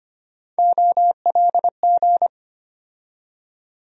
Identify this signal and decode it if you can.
Morse code, heard as a keyed tone, about 25 wpm, 700 Hz. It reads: OLG
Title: USA Call Sign Postfixes - 3 Letters 25wpm